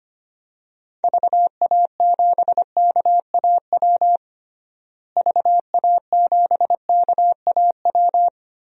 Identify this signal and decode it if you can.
Morse code, heard as a keyed tone, about 25 wpm, 700 Hz. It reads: VA7KAW VA7KAW
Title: DXCC-100 Call Signs - 6 Characters - Repeated 2x - 25wpm